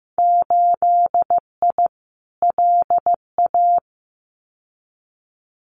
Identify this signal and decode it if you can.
Morse code, heard as a keyed tone, about 15 wpm, 700 Hz. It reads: 8I LA
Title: Winter Field Day Exchanges Encore 15wpm